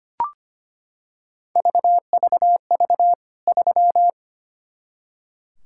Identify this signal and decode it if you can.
Morse code, heard as a keyed tone, about 25 wpm, 700 Hz. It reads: VVV 3
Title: K1USN SST QSO Practice 25wpm